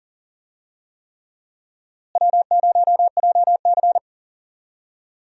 Morse code, heard as a keyed tone, about 40 wpm, 700 Hz. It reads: W0JC
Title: USA Call Signs - 4 Characters - Repeated 3x - 40wpm